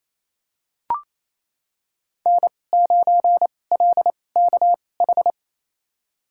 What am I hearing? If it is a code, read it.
Morse code, heard as a keyed tone, about 28 wpm, spaced as for 20 wpm, 700 Hz. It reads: N9LKH
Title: USA Call Signs - 5 Characters - 2x Character Spacing - 28wpm